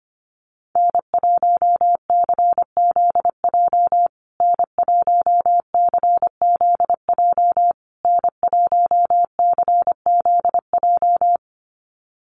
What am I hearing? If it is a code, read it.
Morse code, heard as a keyed tone, about 25 wpm, 700 Hz. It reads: N1CZJ N1CZJ N1CZJ
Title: USA Call Signs Repeated 3x 25wpm